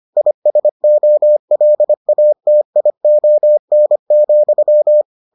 Morse code, heard as a keyed tone, about 25 wpm, 600 Hz. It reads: ISOLATION,